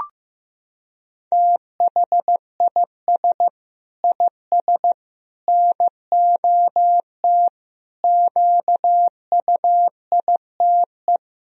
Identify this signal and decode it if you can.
Morse code, heard as a keyed tone, about 15 wpm, 700 Hz. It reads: THIS IS NOT QUITE
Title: Sentences from Top 1000 Words - Review Part 2 - 15wpm